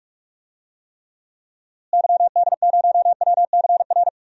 Morse code, heard as a keyed tone, about 45 wpm, 700 Hz. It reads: YD0WCR
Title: DXCC-100 Call Signs - 6 Characters - Repeated 3x - 45wpm